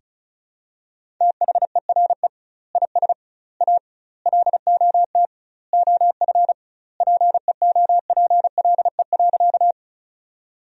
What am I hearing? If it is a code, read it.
Morse code, heard as a keyed tone, about 36 wpm, 700 Hz. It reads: THERE IS A LOT OF PEOPLE.
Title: 2x Word-Spacing - Sentences from Top 500 Words - 35wpm